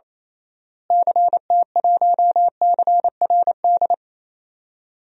Morse code, heard as a keyed tone, about 28 wpm, 700 Hz. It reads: CT1CRD
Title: DXCC-100 Call Signs - 6 Characters - Repeated 2x - 28wpm